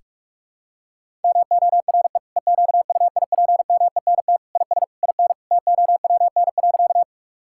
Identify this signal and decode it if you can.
Morse code, heard as a keyed tone, about 45 wpm, 700 Hz. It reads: MORE EQUIPMENT IS IN TOWN.
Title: Sentences from Top 1000 Words - Review Part 5 - 45wpm